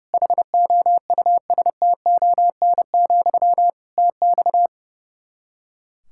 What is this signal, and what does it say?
Morse code, heard as a keyed tone, about 30 wpm, 700 Hz. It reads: HOUSTON, TX
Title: K1USN SST QSO Practice 30wpm